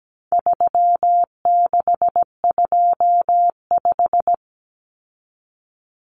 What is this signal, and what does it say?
Morse code, heard as a keyed tone, about 17 wpm, 700 Hz. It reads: 3625
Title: Common Cut Numbers - 4-Digit 17wpm